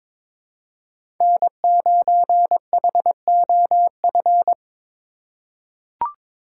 Morse code, heard as a keyed tone, about 23 wpm, 700 Hz. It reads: N9HOF
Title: USA Call Signs - 5 Characters - 3x Character Spacing - 22wpm